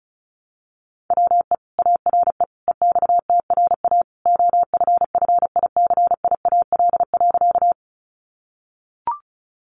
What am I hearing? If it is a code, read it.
Morse code, heard as a keyed tone, about 34 wpm, 700 Hz. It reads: WE ARE EXTRA OFFICIAL.
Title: Sentences from Top 1000 Words 35wpm